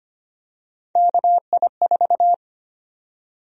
Morse code, heard as a keyed tone, about 25 wpm, 700 Hz. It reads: KI4
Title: USA Call Sign Prefixes - 25wpm